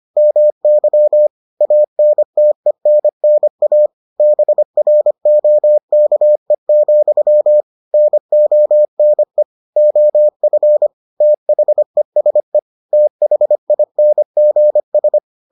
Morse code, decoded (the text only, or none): MY ANTENNA BROKE, NONE OF THESE THINGS